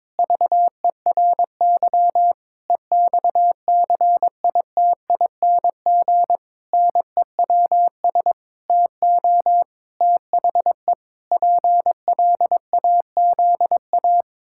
Morse code, heard as a keyed tone, about 22 wpm, 700 Hz. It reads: VERY EXCITING NEWS TO THE PLAZA
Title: Pangram Challenge v2 - 22wpm